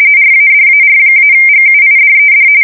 RTTY, about 45.45 baud. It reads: RADAR-GUIDED
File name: fsk170 / battleshipnj